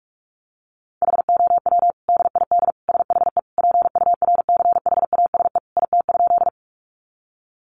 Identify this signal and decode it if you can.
Morse code, heard as a keyed tone, about 44 wpm, 700 Hz. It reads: HOW DID SHE PURCHASE IT?
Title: Sentences from Top 800 Words 45wpm